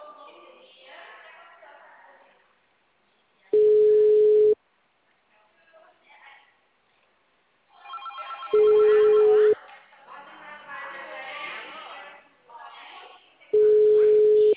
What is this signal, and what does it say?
Ringback tone (Europe, 425 Hz), with speech.
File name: out- -2044- - - .13885